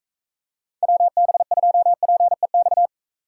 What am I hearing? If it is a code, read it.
Morse code, heard as a keyed tone, about 42 wpm, 700 Hz. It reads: WB2PEX